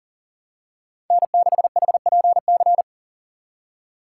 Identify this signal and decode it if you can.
Morse code, heard as a keyed tone, about 40 wpm, 700 Hz. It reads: N6HPC
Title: USA Call Signs - 5 Characters - 3x Character Spacing - 40wpm